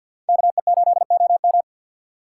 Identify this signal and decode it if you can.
Morse code, heard as a keyed tone, about 62 wpm, 700 Hz. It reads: KE8OM
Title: Speed Racing - USA 2x2 Call Signs 50wpm